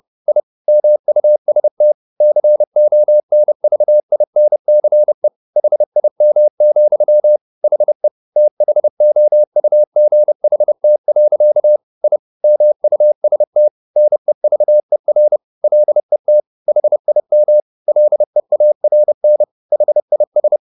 Morse code, heard as a keyed tone, about 30 wpm, 600 Hz. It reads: I MUST CONVINCE HIM, HE THOUGHT. I MUST NEVER LET HIM LEARN HIS
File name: the old man and the sea / the old man and the sea 1 30wpm